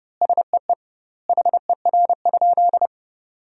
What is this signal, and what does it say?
Morse code, heard as a keyed tone, about 30 wpm, 700 Hz. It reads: SEE HER?
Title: 2x Word-Spacing - Sentences from Top 100 Words - 30wpm